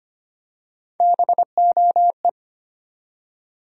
Morse code, heard as a keyed tone, about 25 wpm, 700 Hz. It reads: BOE
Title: USA Call Sign Postfixes - 3 Letters 25wpm